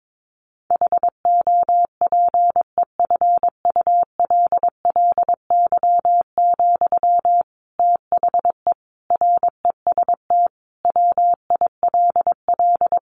Morse code, heard as a keyed tone, about 22 wpm, 700 Hz. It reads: HOPEFULLY, THE REST WILL
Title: Sentences from Top 1500 Words 22wpm